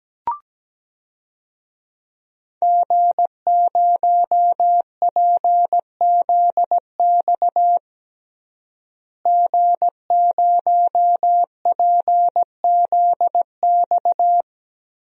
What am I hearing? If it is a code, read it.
Morse code, heard as a keyed tone, about 17 wpm, 700 Hz. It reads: G0PZX G0PZX
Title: DXCC-100 Call Signs - 5 Characters - Repeated 3x - 17wpm